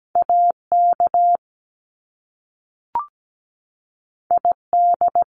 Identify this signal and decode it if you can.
Morse code, heard as a keyed tone, about 17 wpm, 700 Hz. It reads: AK ID
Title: US States 17wpm